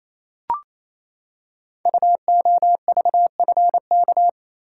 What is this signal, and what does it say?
Morse code, heard as a keyed tone, about 29 wpm, 700 Hz. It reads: UOVFK
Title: Code Groups - 5 Letters - 28wpm